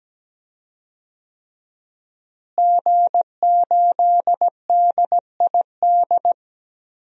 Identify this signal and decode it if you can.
Morse code, heard as a keyed tone, about 17 wpm, 700 Hz. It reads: G8DID
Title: DXCC-100 Call Signs - 5 Characters - Repeated 3x - 17wpm